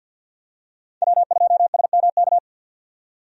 Morse code, heard as a keyed tone, about 52 wpm, 700 Hz. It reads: W2SMK
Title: USA Call Signs - 5 Characters - 50wpm